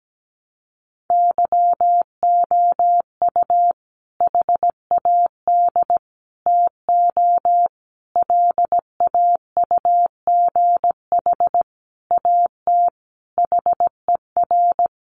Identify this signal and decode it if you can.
Morse code, heard as a keyed tone, about 17 wpm, 700 Hz. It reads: YOU HAD TO LAUGH AT HER
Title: Sentences from Top 600 Words 17wpm